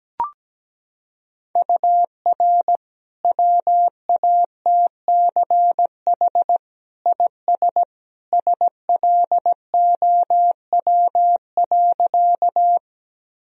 Morse code, heard as a keyed tone, about 17 wpm, 700 Hz. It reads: UR WATCH IS SLOW.
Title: Sentences from Top 400 Words and Abbreviations 17wpm